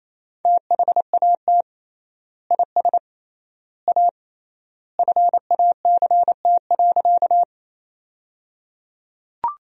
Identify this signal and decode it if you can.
Morse code, heard as a keyed tone, about 27 wpm, 700 Hz. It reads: THAT IS A FACT.
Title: 3x Word-Spacing - Sentences from Top 500 Words - 28wpm